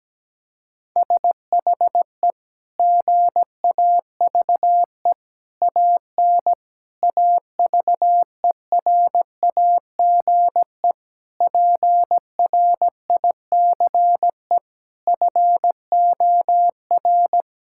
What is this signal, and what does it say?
Morse code, heard as a keyed tone, about 17 wpm, 700 Hz. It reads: SHE GAVE AN AVERAGE PRICE FOR
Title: Sentences from Top 1000 Words - Review Part 6 - 17wpm